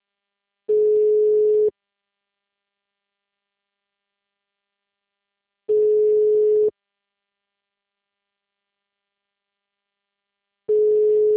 Ringback tone (Europe, 425 Hz).
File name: out- -2030- - - .80467